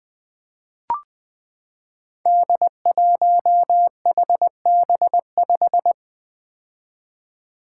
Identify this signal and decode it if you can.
Morse code, heard as a keyed tone, about 20 wpm, 700 Hz. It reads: D1HB5
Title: Five Challenging Letter-Number Combinations 20wpm